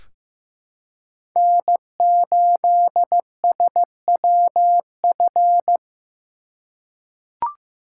Morse code, heard as a keyed tone, about 15 wpm, 700 Hz. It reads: N8SWF
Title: USA Call Signs - 2x Character Spacing - 15wpm